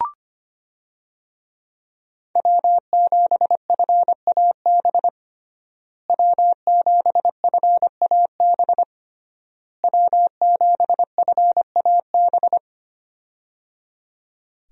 Morse code, heard as a keyed tone, about 25 wpm, 700 Hz. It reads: W7FAB W7FAB W7FAB
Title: USA Call Signs - 5 Characters - Repeated 3x - 25wpm